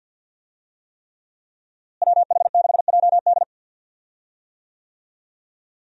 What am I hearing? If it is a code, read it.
Morse code, heard as a keyed tone, about 53 wpm, 700 Hz. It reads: WH6JD
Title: USA Call Signs - 5 Characters - Repeated 3x - 50wpm